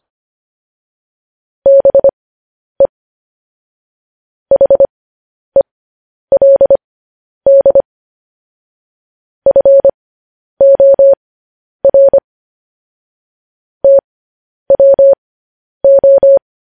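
Morse code, heard as a keyed tone, about 25 wpm, spaced as for 10 wpm, 550 Hz. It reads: BE HELD FOR TWO
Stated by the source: Morse code news headlines, likely sent at 10 WPM.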